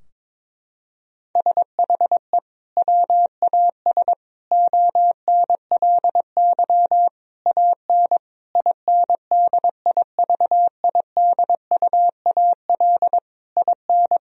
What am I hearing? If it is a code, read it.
Morse code, heard as a keyed tone, about 22 wpm, 700 Hz. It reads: SHE WAS ONLY AN INDIVIDUAL IN
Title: Sentences from Top 1000 Words - Review Part 4 - 22wpm